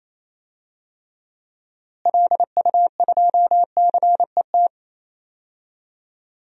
Morse code, heard as a keyed tone, about 28 wpm, 700 Hz. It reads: LU2CET
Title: DXCC-100 Call Signs - 6 Characters - Repeated 2x - 28wpm